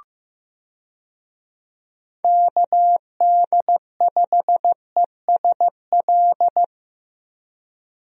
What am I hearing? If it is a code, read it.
Morse code, heard as a keyed tone, about 15 wpm, 700 Hz. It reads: KD5ESL
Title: USA Call Signs - 6 Characters - 15wpm